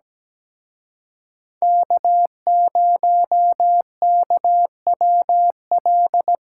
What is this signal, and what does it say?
Morse code, heard as a keyed tone, about 19 wpm, 700 Hz. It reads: K0KWL
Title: Speed Racing - USA 1x3 Call Signs 17wpm